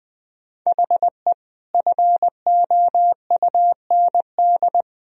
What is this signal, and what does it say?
Morse code, heard as a keyed tone, about 20 wpm, 700 Hz. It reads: HE FOUND
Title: Sentences from Top 1000 Words - Review Part 4 - 20wpm